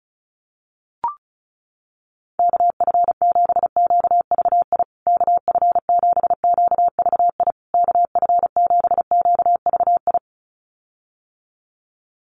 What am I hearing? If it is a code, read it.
Morse code, heard as a keyed tone, about 34 wpm, 700 Hz. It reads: KF7QVI KF7QVI KF7QVI
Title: USA Call Signs Repeated 3x 35wpm